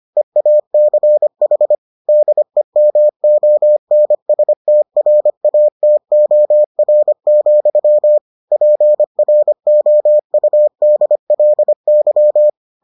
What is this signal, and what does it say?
Morse code, heard as a keyed tone, about 25 wpm, 600 Hz. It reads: EACH DEMONSTRATOR, PROUDLY